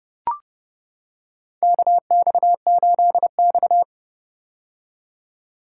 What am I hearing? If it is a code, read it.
Morse code, heard as a keyed tone, about 30 wpm, 700 Hz. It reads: KX8X
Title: K1USN SST QSO Practice - 30wpm